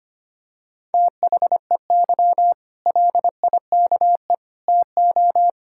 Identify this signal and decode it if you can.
Morse code, heard as a keyed tone, about 25 wpm, 700 Hz. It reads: THEY LIKE TO
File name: Easy Sentences with Hints - 25wpm